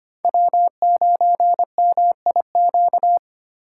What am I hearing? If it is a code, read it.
Morse code, heard as a keyed tone, about 25 wpm, 700 Hz. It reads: W9MIQ
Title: USA Call Signs - 5 Characters - 2x Character Spacing - 25wpm